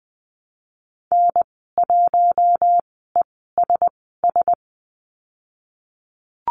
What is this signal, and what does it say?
Morse code, heard as a keyed tone, about 20 wpm, spaced as for 15 wpm, 700 Hz. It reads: N1ESS
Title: 076 Lesson 13g - Letter L - Call Signs 20wpm